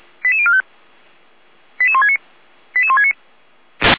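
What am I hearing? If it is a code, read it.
Five-tone selective call (ZVEI), heard as quick runs of five tones, tones of about 70 ms. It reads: 88045 88169 88169
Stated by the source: ZVEI five-tone selective call received on an SDR.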